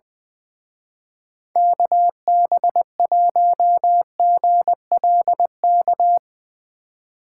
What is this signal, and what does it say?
Morse code, heard as a keyed tone, about 22 wpm, 700 Hz. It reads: KB1GLK
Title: Speed Racing - USA 2x3 Call Signs 20wpm